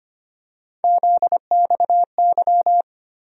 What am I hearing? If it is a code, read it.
Morse code, heard as a keyed tone, about 25 wpm, 700 Hz. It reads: ZXY
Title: USA Call Sign Postfixes - 3 Letters 25wpm